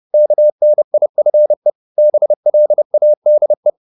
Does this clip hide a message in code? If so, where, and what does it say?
Yes: Morse code from 0.1 to 3.7 s, heard as a keyed tone, about 30 wpm, 600 Hz. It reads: KNIFE BLADE